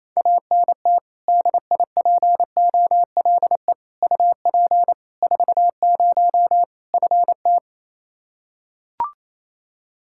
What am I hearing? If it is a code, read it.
Morse code, heard as a keyed tone, about 28 wpm, 700 Hz. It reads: ANT DIPOLE UP 40 FT
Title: Practice QSO - Exchange 2 - Info - 28wpm